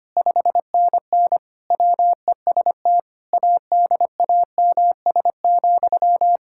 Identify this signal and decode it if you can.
Morse code, heard as a keyed tone, about 25 wpm, 700 Hz. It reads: 5NN WEST ADAMS,
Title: Practice QSO - Exchange 1 - Beginning - 25wpm